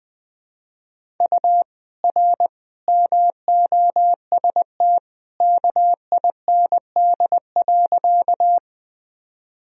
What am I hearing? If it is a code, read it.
Morse code, heard as a keyed tone, about 20 wpm, 700 Hz. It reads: U R MOST KIND.
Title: Sentences from Top 200 Words and Abbreviations 20wpm